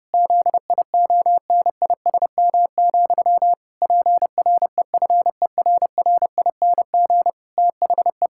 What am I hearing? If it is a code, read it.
Morse code, heard as a keyed tone, about 30 wpm, 700 Hz. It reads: ZIONISM, PREFERRING THE
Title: Pangram Challenge v2 - 30wpm